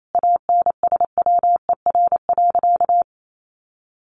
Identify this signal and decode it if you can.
Morse code, heard as a keyed tone, about 28 wpm, 700 Hz. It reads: ANSWER.
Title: Sentences from Top 500 Words 28wpm